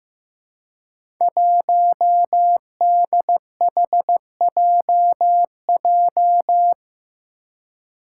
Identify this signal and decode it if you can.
Morse code, heard as a keyed tone, about 15 wpm, 700 Hz. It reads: 1DHJJ
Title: Five Challenging Letter-Number Combinations 15wpm